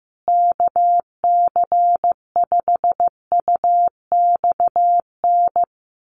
Morse code, heard as a keyed tone, about 15 wpm, 700 Hz. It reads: KC5UXN
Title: USA Call Signs Repeated 2x 15wpm